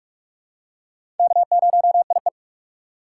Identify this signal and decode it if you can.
Morse code, heard as a keyed tone, about 46 wpm, 700 Hz. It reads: K0IE
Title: USA Call Signs - 4 Characters - 2x Character Spacing - 45wpm